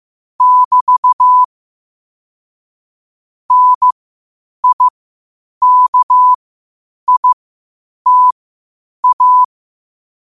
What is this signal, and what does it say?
Morse code, heard as a keyed tone, about 15 wpm, spaced as for 8 wpm, 1000 Hz. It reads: <BT> NIKITA
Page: New quotes every day in morse code at 8 Words per minute.